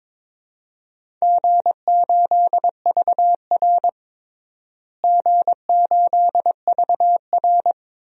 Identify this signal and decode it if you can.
Morse code, heard as a keyed tone, about 22 wpm, 700 Hz. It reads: G8VR G8VR
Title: DXCC-100 Call Signs - 4 Characters - Repeated 2x - 22wpm